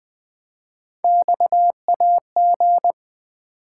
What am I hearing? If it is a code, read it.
Morse code, heard as a keyed tone, about 20 wpm, 700 Hz. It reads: XAG
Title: Three Letter-Number Combinations - 20wpm